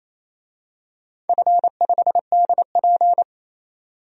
Morse code, heard as a keyed tone, about 28 wpm, 700 Hz. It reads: F5DP
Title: DXCC-100 Call Signs - 4 Characters - 28wpm